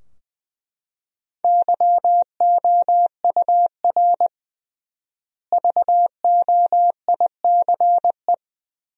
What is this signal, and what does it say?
Morse code, heard as a keyed tone, about 20 wpm, 700 Hz. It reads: YOUR VOICE
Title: 3x Word-Spacing - Sentences from Top 500 Words - 20wpm